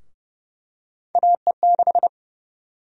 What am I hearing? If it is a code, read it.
Morse code, heard as a keyed tone, about 30 wpm, 700 Hz. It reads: AE6
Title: USA Call Sign Prefixes - Repeated 2x - 30wpm